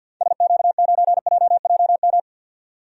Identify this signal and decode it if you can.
Morse code, heard as a keyed tone, about 50 wpm, 700 Hz. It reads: SQ9JJM
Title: DXCC-100 Call Signs - 6 Characters - 50wpm